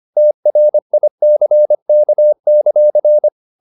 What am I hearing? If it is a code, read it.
Morse code, heard as a keyed tone, about 25 wpm, 600 Hz. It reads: TRICK;